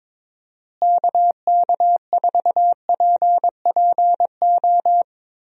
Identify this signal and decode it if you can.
Morse code, heard as a keyed tone, about 22 wpm, 700 Hz. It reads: KK4PPO
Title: USA Call Signs - 6 Characters - Repeated 2x - 22wpm